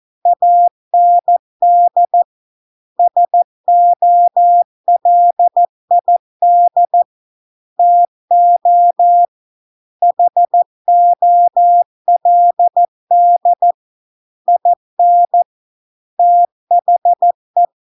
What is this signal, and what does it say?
Morse code, heard as a keyed tone, about 14 wpm, 700 Hz. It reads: AND SOLID TO HOLD IN THE